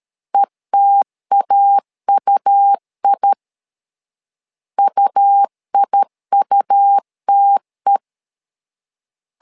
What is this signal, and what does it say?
Morse code, heard as a keyed tone, about 13 wpm, 800 Hz. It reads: ETAUI UIUTE
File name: LEZIONI / 2 UHDN / 2 U 12wpm 1338